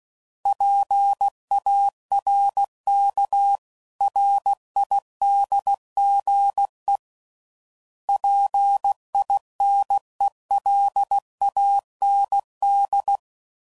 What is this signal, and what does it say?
Morse code, heard as a keyed tone, about 16 wpm, 800 Hz. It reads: PARKRIDGE PINELAND
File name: CW Code Groups @15WPM / fromSM's / Cities